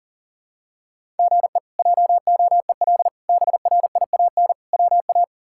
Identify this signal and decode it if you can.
Morse code, heard as a keyed tone, about 40 wpm, 700 Hz. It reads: GE JOEL BRIAN WA
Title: K1USN SST QSO Practice - 40wpm